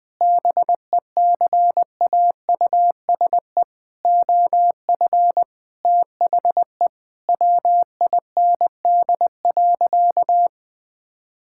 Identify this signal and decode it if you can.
Morse code, heard as a keyed tone, about 21 wpm, 700 Hz. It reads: BECAUSE OF THE WIND.
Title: Sentences with 3-Word Phrases - Spoken Beginning - 20wpm